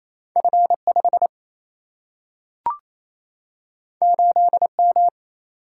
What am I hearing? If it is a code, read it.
Morse code, heard as a keyed tone, about 28 wpm, 700 Hz. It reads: F 5 8 M
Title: Single-Letter Single-Number 28wpm